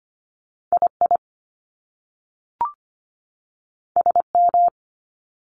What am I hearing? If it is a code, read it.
Morse code, heard as a keyed tone, about 25 wpm, 700 Hz. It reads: I I S M
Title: Top Two Letter Combinations in Words 25wpm